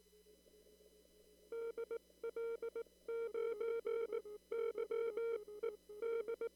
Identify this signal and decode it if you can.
Morse code, heard as a keyed tone, about 18 wpm, spaced as for 15 wpm, 920 Hz. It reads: DL9YED
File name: CW / kt5j wpx cw-op-k5pi